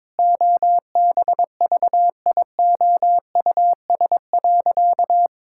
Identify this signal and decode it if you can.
Morse code, heard as a keyed tone, about 21 wpm, 700 Hz. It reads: OBVIOUS.
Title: Sentences from Top 1600 Words 22wpm